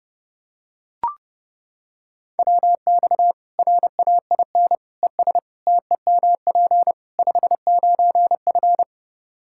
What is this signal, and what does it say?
Morse code, heard as a keyed tone, about 30 wpm, 700 Hz. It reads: WX RAIN ES TEMP 59F
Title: Practice QSO - Exchange 2 - Info - 30wpm